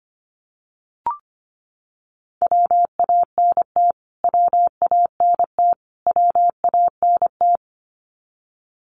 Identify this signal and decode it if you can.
Morse code, heard as a keyed tone, about 25 wpm, 700 Hz. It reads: WANT WANT WANT
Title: Top 500 Words Repeated 25wpm